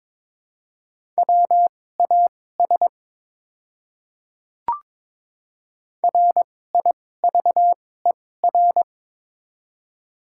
Morse code, heard as a keyed tone, about 22 wpm, spaced as for 16 wpm, 700 Hz. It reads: WAS RIVER
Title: 2x Character Spacing - Top 300 Words - 22wpm